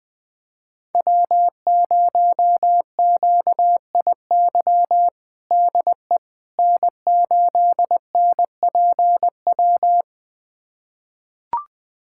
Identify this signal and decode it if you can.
Morse code, heard as a keyed tone, about 21 wpm, 700 Hz. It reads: W0QIY DE N8NPW
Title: USA Callsign DE Callsign - 5 Characters - 20wpm